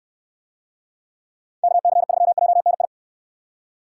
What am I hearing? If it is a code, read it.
Morse code, heard as a keyed tone, about 66 wpm, 700 Hz. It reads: KC3JNI